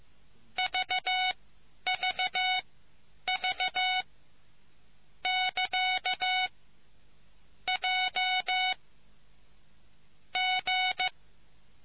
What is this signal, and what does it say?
Morse code, heard as a keyed tone, about 15 wpm, spaced as for 9 wpm, 740 Hz. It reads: VVV <KA> J G